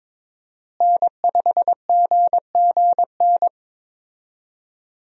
Morse code, heard as a keyed tone, about 21 wpm, 700 Hz. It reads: N5GGN
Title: USA Call Signs - 5 Characters - Repeated 3x - 22wpm